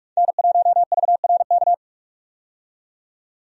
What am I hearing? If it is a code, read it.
Morse code, heard as a keyed tone, about 46 wpm, 700 Hz. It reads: N1VRK
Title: USA Call Signs - 5 Characters - 45wpm